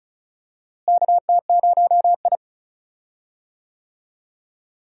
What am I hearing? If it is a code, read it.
Morse code, heard as a keyed tone, about 34 wpm, 700 Hz. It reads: KT0I
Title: USA Call Signs - 4 Characters - Repeated 2x - 35wpm